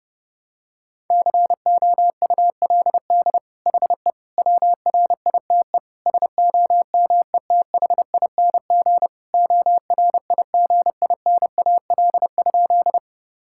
Morse code, heard as a keyed tone, about 32 wpm, 700 Hz. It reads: COULD HE WRITE SOMETHING ORIGINAL?
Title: Sentences from Top 1000 Words - Review Part 6 - 30wpm